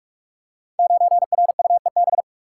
Morse code, heard as a keyed tone, about 44 wpm, 700 Hz. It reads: 9RUEB
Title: Code Groups - 5 Letter:Numbers - 45wpm